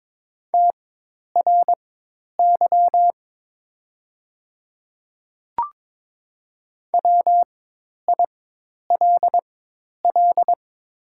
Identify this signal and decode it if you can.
Morse code, heard as a keyed tone, about 22 wpm, spaced as for 10 wpm, 700 Hz. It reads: TRY WILL
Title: 4x Character Spacing - Top 300 Words - 22wpm